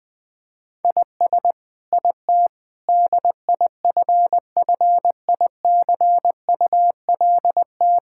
Morse code, heard as a keyed tone, about 19 wpm, 700 Hz. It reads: IS IT DIFFICULT
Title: Sentences from Top 1000 Words - Review Part 6 - 20wpm